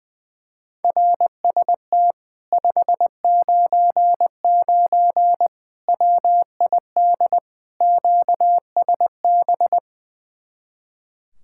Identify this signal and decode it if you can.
Morse code, heard as a keyed tone, about 21 wpm, 700 Hz. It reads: RST 599 WID QSB
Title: Speed-Racing Common QSO Phrases Encore - 20wpm